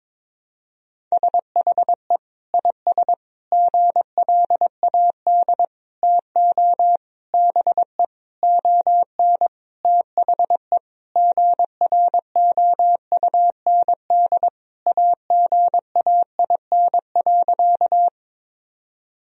Morse code, heard as a keyed tone, about 22 wpm, 700 Hz. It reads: SHE IS GLAD TO BE ON THE GROUND AGAIN.
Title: Sentences from Top 1000 Words - Review Part 2 - 22wpm